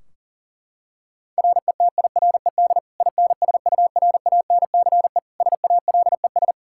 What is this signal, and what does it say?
Morse code, heard as a keyed tone, about 40 wpm, 700 Hz. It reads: RETIRED INSURANCE SALES
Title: K1USN SST QSO Practice 40wpm